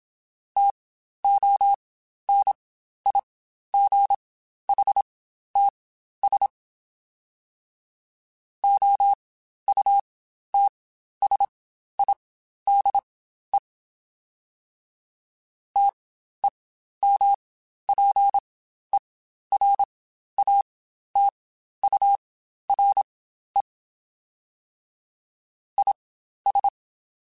Morse code, heard as a keyed tone, about 27 wpm, spaced as for 12 wpm, 800 Hz. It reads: TONIGHTS OUTSIDE TEMPERATURE IS
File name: Spaced Characters / Spaced Characters-mp3s / Weather4 21 sp @25